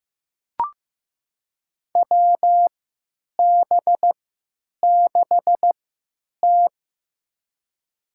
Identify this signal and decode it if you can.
Morse code, heard as a keyed tone, about 15 wpm, spaced as for 8 wpm, 700 Hz. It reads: WB6T
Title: USA Call Signs - 3x Character Spacing - 15wpm